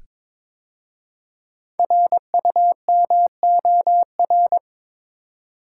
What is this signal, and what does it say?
Morse code, heard as a keyed tone, about 22 wpm, 700 Hz. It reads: RUMOR
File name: 5 Letter Words - First 3 Letters Given - 22wpm